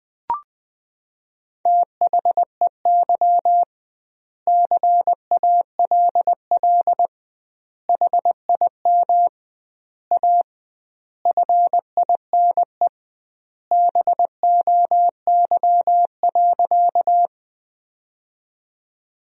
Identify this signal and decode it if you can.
Morse code, heard as a keyed tone, about 21 wpm, 700 Hz. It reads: THEY CALL HIM A FINE BOY.
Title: 2x Word-Spacing - Sentences from Top 500 Words - 20wpm